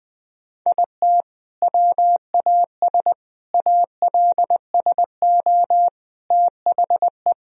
Morse code, heard as a keyed tone, about 21 wpm, 700 Hz. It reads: IT WAS ALSO THE
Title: Sentences from Top 1000 Words - Review Part 1 - 20wpm